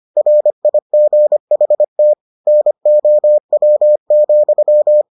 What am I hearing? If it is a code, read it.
Morse code, heard as a keyed tone, about 25 wpm, 600 Hz. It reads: RIGHT NOW,